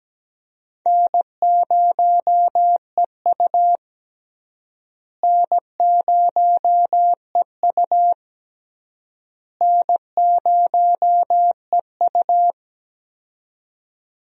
Morse code, heard as a keyed tone, about 17 wpm, 700 Hz. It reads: N0EU N0EU N0EU
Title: USA Call Signs - 4 Characters - Repeated 3x - 17wpm